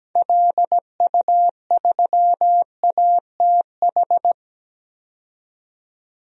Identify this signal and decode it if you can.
Morse code, heard as a keyed tone, about 17 wpm, 700 Hz. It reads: LU3ATH
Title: DXCC-100 Call Signs - 6 Characters - 17wpm